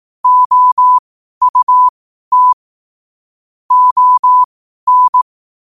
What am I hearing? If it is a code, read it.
Morse code, heard as a keyed tone, about 18 wpm, spaced as for 12 wpm, 1000 Hz. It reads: OUT ON
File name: qotdfiles / 12wpm